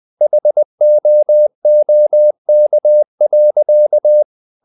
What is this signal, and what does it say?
Morse code, heard as a keyed tone, about 20 wpm, 600 Hz. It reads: HOOK.
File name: the old man and the sea part 2 20wpm